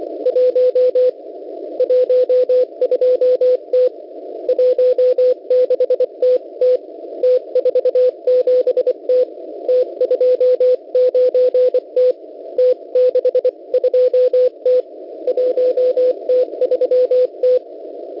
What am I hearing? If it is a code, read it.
Morse code, heard as a keyed tone, about 25 wpm, 500 Hz. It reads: 1 12T 16TT T47T T29T T62T 1T3T
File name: Unid- -0700utc-4735khz